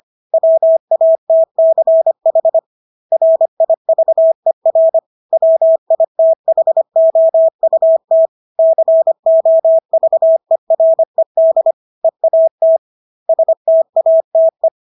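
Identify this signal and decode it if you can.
Morse code, heard as a keyed tone, about 25 wpm, 650 Hz. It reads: WATCH RIVER WITHOUT COVERED EAT STATE
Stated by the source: Morse practice of English words at 25 wpm from CW med Gnister.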